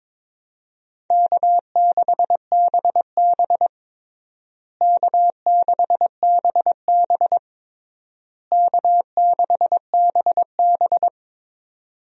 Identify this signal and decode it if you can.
Morse code, heard as a keyed tone, about 22 wpm, 700 Hz. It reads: K6BB K6BB K6BB
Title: USA Call Signs - 4 Characters - Repeated 3x - 22wpm